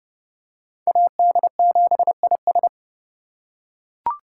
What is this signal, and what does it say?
Morse code, heard as a keyed tone, about 28 wpm, 700 Hz. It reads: AD7IS